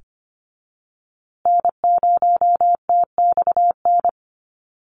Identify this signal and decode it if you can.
Morse code, heard as a keyed tone, about 25 wpm, 700 Hz. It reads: N0TXN
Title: USA Call Signs Repeated 2x 25wpm